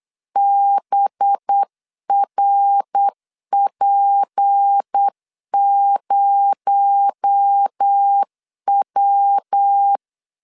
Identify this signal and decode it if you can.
Morse code, heard as a keyed tone, about 8 wpm, 800 Hz. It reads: BRP0W
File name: LEZIONI / 7 GF9Z / 7 GF 8wpm 3108